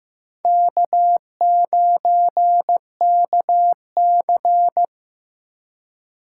Morse code, heard as a keyed tone, about 17 wpm, 700 Hz. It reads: K9KC
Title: Speed Racing - USA 1x2 Call Signs 15wpm